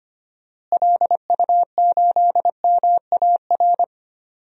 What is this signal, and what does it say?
Morse code, heard as a keyed tone, about 25 wpm, 700 Hz. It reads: LU8MAR
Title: DXCC-100 Call Signs - 6 Characters - Repeated 2x - 25wpm